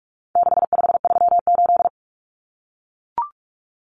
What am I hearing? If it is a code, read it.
Morse code, heard as a keyed tone, about 44 wpm, 700 Hz. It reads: 6538
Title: Common Cut Numbers - 4-Digit 45wpm